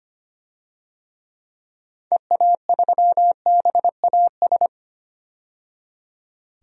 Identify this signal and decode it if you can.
Morse code, heard as a keyed tone, about 25 wpm, 700 Hz. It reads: EA3BAS
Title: DXCC-100 Call Signs - 6 Characters - 25wpm